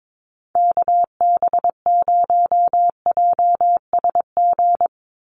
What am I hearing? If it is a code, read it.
Morse code, heard as a keyed tone, about 22 wpm, 700 Hz. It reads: KB0JSG
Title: USA Call Signs 22wpm